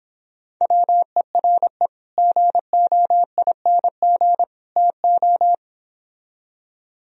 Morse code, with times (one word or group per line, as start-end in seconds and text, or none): 0.6-1.9 WERE
2.2-4.4 GOING
4.8-5.5 TO